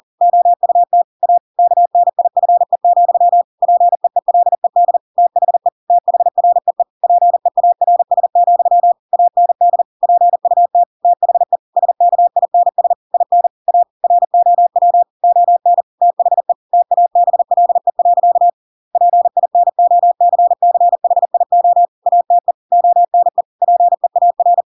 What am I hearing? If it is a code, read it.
Morse code, heard as a keyed tone, about 40 wpm, 700 Hz. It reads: OUT A KNIFE, PEELED THE THREE PEARS, AND PUT THE SKINS IN A ROW ON THE TABLE. PINOCCHIO ATE ONE PEAR
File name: Pinocchio / Pinocchio ch7 40 40 ik1hge